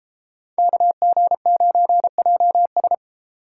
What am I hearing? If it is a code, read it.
Morse code, heard as a keyed tone, about 36 wpm, 700 Hz. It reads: KG9JS